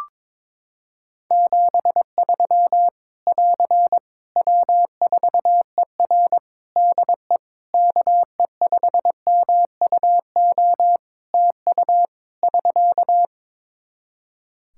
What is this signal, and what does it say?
Morse code, heard as a keyed tone, about 21 wpm, 700 Hz. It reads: 73 <AR> W4ER DE KE5MUO TU <SK>
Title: K1USN SST QSO Practice 22wpm